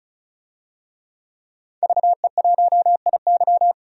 Morse code, heard as a keyed tone, about 36 wpm, 700 Hz. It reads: VE1IY
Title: DXCC-100 Call Signs - 5 Characters - 35wpm